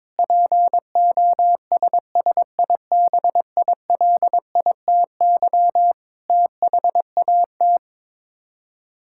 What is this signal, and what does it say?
Morse code, heard as a keyed tone, about 22 wpm, 700 Hz. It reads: POSSIBILITY THAT
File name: Sets of 9 Words - 22wpm